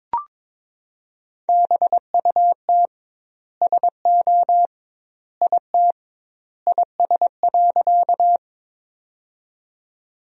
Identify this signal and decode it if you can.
Morse code, heard as a keyed tone, about 21 wpm, 700 Hz. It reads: BUT SO IT IS.
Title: 2x Word-Spacing - Sentences from Top 100 Words - 22wpm